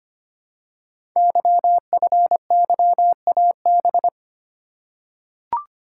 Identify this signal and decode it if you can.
Morse code, heard as a keyed tone, about 25 wpm, 700 Hz. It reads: YFYAB
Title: Code Groups - 5 Letter:Numbers - 25wpm